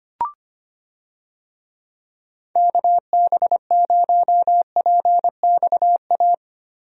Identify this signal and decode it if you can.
Morse code, heard as a keyed tone, about 25 wpm, 700 Hz. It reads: KB0PXA
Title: USA Call Signs - 6 Characters - Repeated 3x - 25wpm